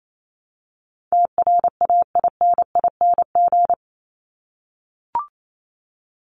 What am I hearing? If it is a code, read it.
Morse code, heard as a keyed tone, about 28 wpm, 700 Hz. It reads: TRAINING
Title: Top 501-750 Words 28wpm